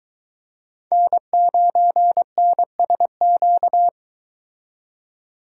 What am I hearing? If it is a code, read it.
Morse code, heard as a keyed tone, about 24 wpm, 700 Hz. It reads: N9NSQ